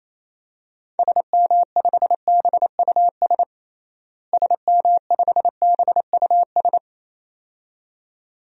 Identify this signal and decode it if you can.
Morse code, heard as a keyed tone, about 27 wpm, 700 Hz. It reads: SM5BUS SM5BUS
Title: DXCC-100 Call Signs - 6 Characters - Repeated 2x - 28wpm